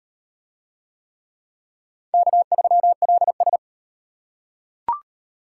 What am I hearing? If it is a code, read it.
Morse code, heard as a keyed tone, about 33 wpm, 700 Hz. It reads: K3LS